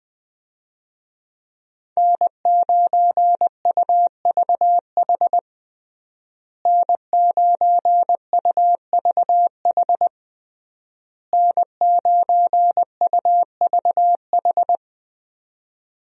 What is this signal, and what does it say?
Morse code, heard as a keyed tone, about 20 wpm, 700 Hz. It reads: N9UVH N9UVH N9UVH
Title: USA Call Signs - 5 Characters - Repeated 3x - 20wpm